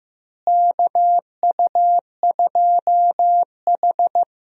Morse code, heard as a keyed tone, about 15 wpm, 700 Hz. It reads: KU2H
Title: USA Call Signs - 4 Characters - Repeated 3x - 15wpm